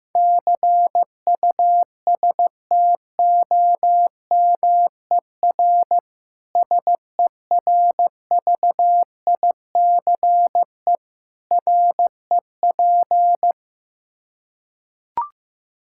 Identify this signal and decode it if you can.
Morse code, heard as a keyed tone, about 15 wpm, 700 Hz. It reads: CUSTOMER SERVICE REP
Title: K1USN SST QSO Practice 15wpm